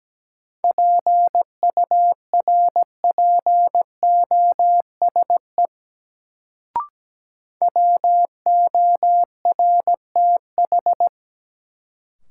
Morse code, heard as a keyed tone, about 17 wpm, 700 Hz. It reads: PURPOSE WORTH
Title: Sentences from Top 800 Words - New Word Drill - 17wpm